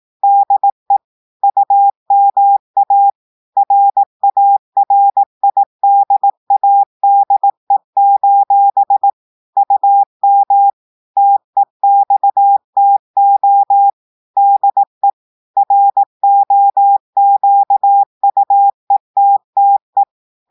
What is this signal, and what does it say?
Morse code, heard as a keyed tone, about 18 wpm, 800 Hz. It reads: DE UMA RARIDADE: UM TEXTO DE ROQUETTE
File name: audios / portugais / portugais texte en clair 1 vitesse 18 mots minutes